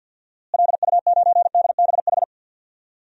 Morse code, heard as a keyed tone, about 47 wpm, 700 Hz. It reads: LU9DBH
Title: DXCC-100 Call Signs - 6 Characters - Repeated 2x - 50wpm